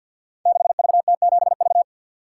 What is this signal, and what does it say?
Morse code, heard as a keyed tone, about 51 wpm, 700 Hz. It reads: 6VT7V
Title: Code Groups - 5 Letter:Numbers - 50wpm